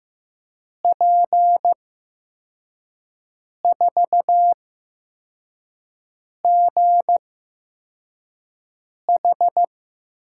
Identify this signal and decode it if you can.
Morse code, heard as a keyed tone, about 15 wpm, spaced as for 4 wpm, 700 Hz. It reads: P4GH
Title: USA Call Signs - 5 Characters - 8x Character Spacing - 15wpm